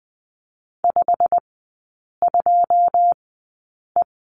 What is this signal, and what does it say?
Morse code, heard as a keyed tone, about 20 wpm, spaced as for 8 wpm, 700 Hz. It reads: 52E
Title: 087 Lesson 15e - Number 5 - Three Characters 20wpm